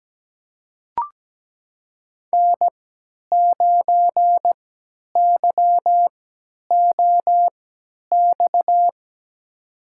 Morse code, heard as a keyed tone, about 17 wpm, spaced as for 10 wpm, 700 Hz. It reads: N9YOX
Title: USA Call Signs - 5 Characters - 3x Character Spacing - 17wpm